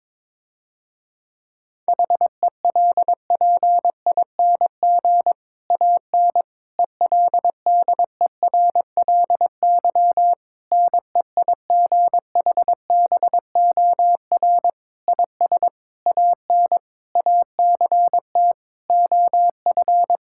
Morse code, heard as a keyed tone, about 22 wpm, 700 Hz. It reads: HELPING AN ELDERLY NEIGHBOR IS AN ACT OF
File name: Difficult Sentences with Hints - 22wpm